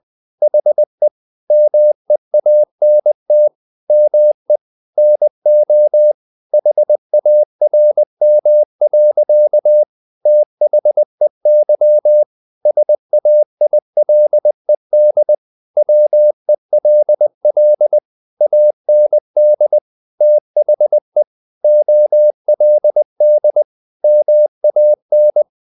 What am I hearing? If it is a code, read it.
Morse code, heard as a keyed tone, about 20 wpm, 600 Hz. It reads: HE MEANT ME NO HARM. THEY SAILED WELL AND THE OLD MAN
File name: the old man and the sea part 2 20wpm